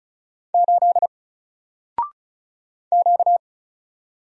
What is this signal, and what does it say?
Morse code, heard as a keyed tone, about 34 wpm, 700 Hz. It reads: 8 Q
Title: Single Letter-Number Alt 35wpm